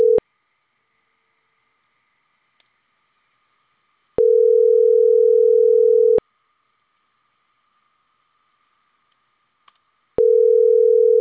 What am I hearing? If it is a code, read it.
Ringback tone (North America), 440 + 480 Hz.